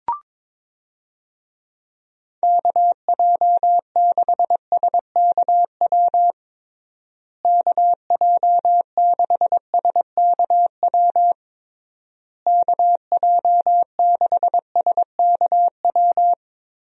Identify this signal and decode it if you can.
Morse code, heard as a keyed tone, about 21 wpm, 700 Hz. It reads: KJ6SKW KJ6SKW KJ6SKW
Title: USA Call Signs - 6 Characters - Repeated 3x - 22wpm